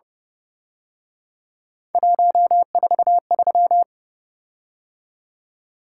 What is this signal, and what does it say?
Morse code, heard as a keyed tone, about 30 wpm, 700 Hz. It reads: 143